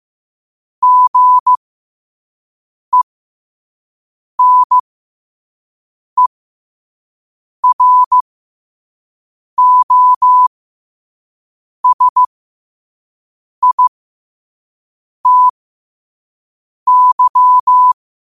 Morse code, heard as a keyed tone, about 15 wpm, spaced as for 5 wpm, 1000 Hz. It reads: GENEROSITY
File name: qotdfiles / 5wpm